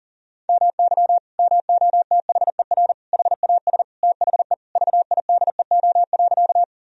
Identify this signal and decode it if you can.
Morse code, heard as a keyed tone, about 40 wpm, 700 Hz. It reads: MY MOTHER HAS THE VIDEO.
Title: Sentences from Top 1000 Words - Review Part 4 - 40wpm